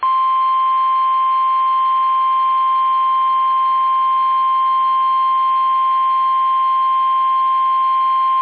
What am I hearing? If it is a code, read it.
Emergency Alert System (SAME) tones (the 1050 Hz attention tone). It is heard from 0.8 to 8.2 s.